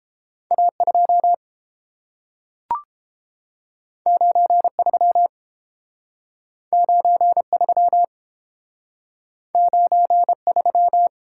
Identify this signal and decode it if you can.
Morse code, heard as a keyed tone, about 30 wpm, 700 Hz. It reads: A2 93 93 93